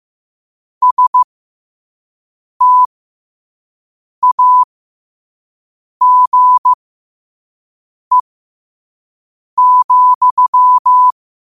Morse code, heard as a keyed tone, about 15 wpm, spaced as for 5 wpm, 1000 Hz. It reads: STAGE,
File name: qotdfiles / 5wpm